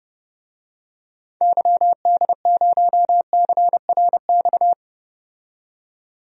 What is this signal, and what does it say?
Morse code, heard as a keyed tone, about 30 wpm, 700 Hz. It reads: YD0CRX
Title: DXCC-100 Call Signs - 6 Characters - 30wpm